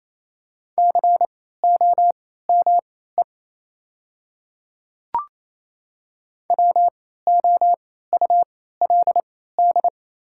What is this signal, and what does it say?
Morse code, heard as a keyed tone, about 28 wpm, spaced as for 16 wpm, 690 Hz. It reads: COME WOULD
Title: 3x Character Spacing - Top 100 Words - 28wpm